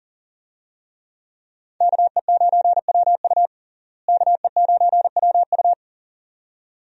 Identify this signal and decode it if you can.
Morse code, heard as a keyed tone, about 40 wpm, 700 Hz. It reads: KE9WU KE9WU
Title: USA Call Signs - 5 Characters - Repeated 2x - 40wpm